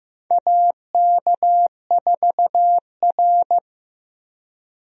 Morse code, heard as a keyed tone, about 15 wpm, 700 Hz. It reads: AK4R
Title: USA Call Signs - 4 Characters - 15wpm